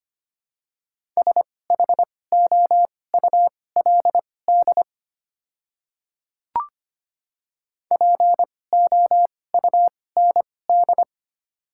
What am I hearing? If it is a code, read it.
Morse code, heard as a keyed tone, about 25 wpm, spaced as for 18 wpm, 700 Hz. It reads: SHOULD POUND
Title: 2x Character Spacing - Top 500 Words - 25wpm